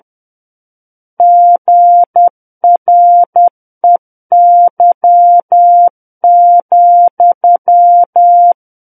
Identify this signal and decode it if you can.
Morse code, heard as a keyed tone, about 10 wpm, 700 Hz. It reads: GREY,
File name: Book One / 01-Chapter 1 - 10 WPM